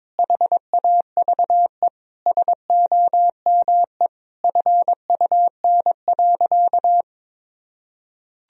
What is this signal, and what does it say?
Morse code, heard as a keyed tone, about 22 wpm, 700 Hz. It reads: HAVE SOME FUN.
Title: Sentences from Top 1000 Words - Review Part 4 - 22wpm